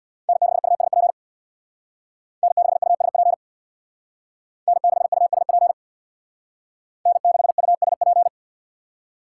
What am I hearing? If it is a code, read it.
Morse code, heard as a keyed tone, about 63 wpm, 700 Hz. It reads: N6USP N6USP N6USP N6USP